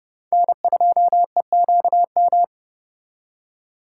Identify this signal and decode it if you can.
Morse code, heard as a keyed tone, about 37 wpm, 700 Hz. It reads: N2EQM
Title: Speed Racing - USA 1x3 Call Signs 30wpm